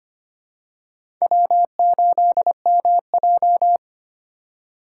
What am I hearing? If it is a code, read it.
Morse code, heard as a keyed tone, about 25 wpm, 700 Hz. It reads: W8MJ
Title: USA Call Signs - 4 Characters - 4x Character Spacing - 25wpm